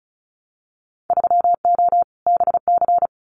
Morse code, heard as a keyed tone, about 36 wpm, 700 Hz. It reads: 3O BC
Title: Winter Field Day Exchanges Encore 35wpm